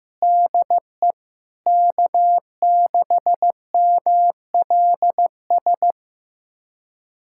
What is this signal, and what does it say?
Morse code, heard as a keyed tone, about 15 wpm, 700 Hz. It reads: DE K6MLS
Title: USA Callsign DE Callsign 15wpm